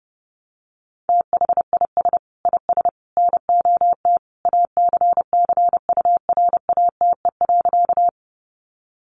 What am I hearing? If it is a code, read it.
Morse code, heard as a keyed tone, about 28 wpm, 700 Hz. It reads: THIS IS NOT ACCURATE.
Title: Sentences from Top 1200 Words 30wpm